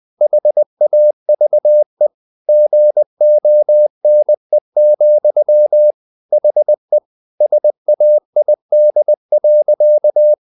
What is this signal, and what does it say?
Morse code, heard as a keyed tone, about 20 wpm, 600 Hz. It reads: HAVE GONE, HE SAID.
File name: the old man and the sea / the old man and the sea part 1 20wpm